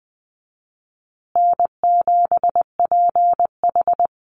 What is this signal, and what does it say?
Morse code, heard as a keyed tone, about 20 wpm, 700 Hz. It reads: N7PH
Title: USA Call Signs Repeated 3x 20wpm